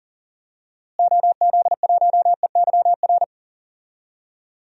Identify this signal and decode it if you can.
Morse code, heard as a keyed tone, about 37 wpm, 700 Hz. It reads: OZ1EYR
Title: DXCC-100 Call Signs - 6 Characters - Repeated 2x - 40wpm